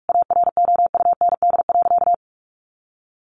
Morse code, heard as a keyed tone, about 44 wpm, 700 Hz. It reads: AROUND.
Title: Additional Easy Phrases 45wpm